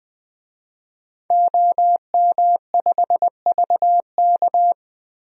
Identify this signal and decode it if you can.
Morse code, heard as a keyed tone, about 20 wpm, 700 Hz. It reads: OM5VK
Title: DXCC-100 Call Signs - 5 Characters - 20wpm